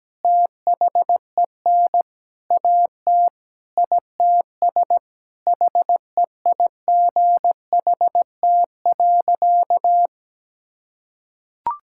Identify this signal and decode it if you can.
Morse code, heard as a keyed tone, about 17 wpm, 700 Hz. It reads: THEN AT ITS HEIGHT.
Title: Sentences from Top 1600 Words 17wpm